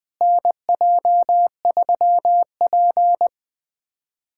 Morse code, heard as a keyed tone, about 21 wpm, 700 Hz. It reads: NJ3P
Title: USA Callsign DE Callsign - 4 Characters - 20wpm